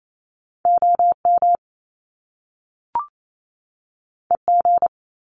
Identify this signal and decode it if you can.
Morse code, heard as a keyed tone, about 28 wpm, 700 Hz. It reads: OM EG
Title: Top Two Letter Combinations in Words 28wpm